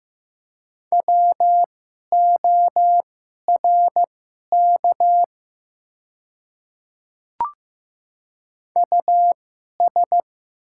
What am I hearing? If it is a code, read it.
Morse code, heard as a keyed tone, about 15 wpm, spaced as for 11 wpm, 700 Hz. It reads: WORK US
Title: 2x Character Spacing - Top 200 Words - 15wpm